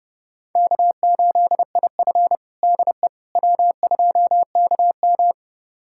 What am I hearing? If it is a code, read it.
Morse code, heard as a keyed tone, about 32 wpm, 700 Hz. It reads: K8IF DE W2KM
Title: USA Callsign DE Callsign - 4 Characters - 30wpm